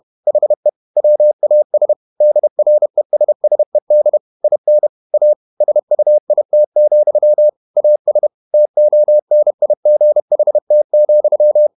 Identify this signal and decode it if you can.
Morse code, heard as a keyed tone, about 31 wpm, 600 Hz. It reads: HE WAS DRESSED IN A SUIT, AS TONIGHT,
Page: Never 31wpm | CW med Gnister